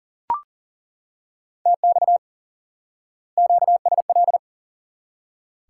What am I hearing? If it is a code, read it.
Morse code, heard as a keyed tone, about 38 wpm, 700 Hz. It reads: TX QSL
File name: IQE Course 3 - Lesson 89 - YAGI - Two Elements - 40wpm